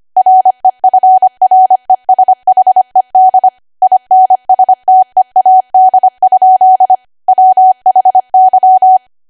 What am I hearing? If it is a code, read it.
Morse code, heard as a keyed tone, about 25 wpm, 750 Hz. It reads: REFRESHED INSTEAD? WHY
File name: cwa / W1AW-Level3 / 25WPM